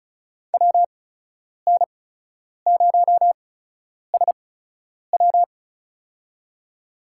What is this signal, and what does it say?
Morse code, heard as a keyed tone, about 36 wpm, spaced as for 10 wpm, 700 Hz. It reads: WN0SW
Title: USA Call Signs - 5 Characters - 8x Character Spacing - 35wpm